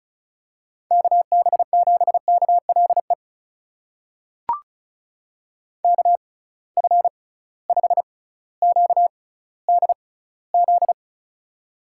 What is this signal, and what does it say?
Morse code, heard as a keyed tone, about 36 wpm, spaced as for 18 wpm, 690 Hz. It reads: KB7KLE KF5QDZ
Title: USA Call Signs - 6 Characters - 6x Character Spacing - 35wpm